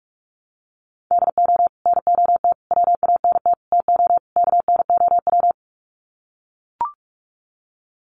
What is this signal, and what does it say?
Morse code, heard as a keyed tone, about 44 wpm, 700 Hz. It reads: DO NOT WANT TO KNOW
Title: Sets of 5 Words Encore 45wpm